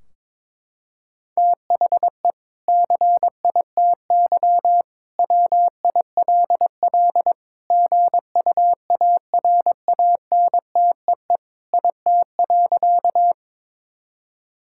Morse code, heard as a keyed tone, about 21 wpm, 700 Hz. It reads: THE CITY WILL GUARANTEE IT.
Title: Sentences from Top 1600 Words 22wpm